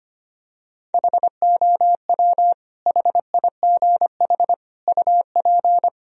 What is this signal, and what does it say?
Morse code, heard as a keyed tone, about 25 wpm, 700 Hz. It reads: HOW HIGH UP
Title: Sentences from Top 1000 Words - Review Part 1 - 25wpm